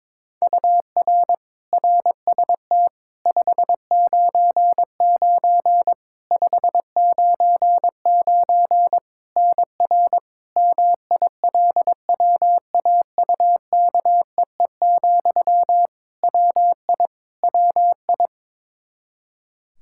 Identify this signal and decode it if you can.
Morse code, heard as a keyed tone, about 22 wpm, 700 Hz. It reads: UR RST 599 599 NR MILWAUKEE, WI WI
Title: Practice QSO - Exchange 1 - Beginning - 22wpm